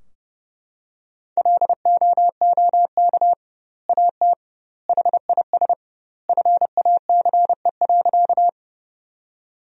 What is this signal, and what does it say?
Morse code, heard as a keyed tone, about 30 wpm, 700 Hz. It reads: LOOK AT HIS FACE.
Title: 2x Word-Spacing - Sentences from Top 300 Words - 30wpm